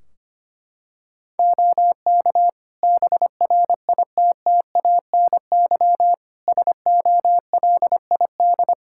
Morse code, heard as a keyed tone, about 25 wpm, 700 Hz. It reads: OK BRITTANY SOLID
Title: K1USN SST QSO Practice 25wpm